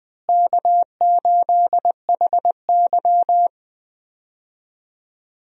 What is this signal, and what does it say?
Morse code, heard as a keyed tone, about 20 wpm, 700 Hz. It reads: K8HY
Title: USA Call Signs - 4 Characters - Repeated 2x - 20wpm